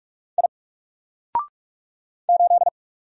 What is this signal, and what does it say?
Morse code, heard as a keyed tone, about 46 wpm, 700 Hz. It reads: I 8
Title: Single Letter-Number 45wpm